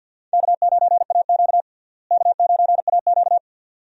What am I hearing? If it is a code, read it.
Morse code, heard as a keyed tone, about 50 wpm, 700 Hz. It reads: K9AQ K9AQ
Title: USA Call Signs - 4 Characters - Repeated 2x - 50wpm